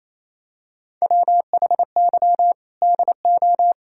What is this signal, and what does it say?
Morse code, heard as a keyed tone, about 28 wpm, 700 Hz. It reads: WHY DO
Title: Sentences from Top 1000 Words - Review Part 6 - 28wpm